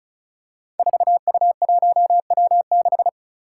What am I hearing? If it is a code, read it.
Morse code, heard as a keyed tone, about 36 wpm, 700 Hz. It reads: 4U1W6
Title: Five Challenging Letter-Number Combinations 35wpm